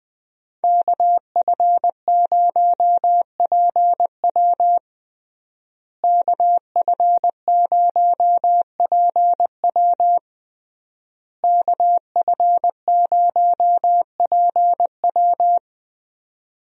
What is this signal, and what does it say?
Morse code, heard as a keyed tone, about 20 wpm, 700 Hz. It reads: KF0PW KF0PW KF0PW
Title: USA Call Signs - 5 Characters - Repeated 3x - 20wpm